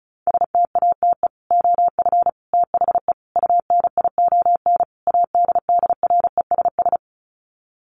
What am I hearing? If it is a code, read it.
Morse code, heard as a keyed tone, about 34 wpm, 700 Hz. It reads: STATE OF THE UNION ADDRESS
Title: Sets of 5 Words Encore 35wpm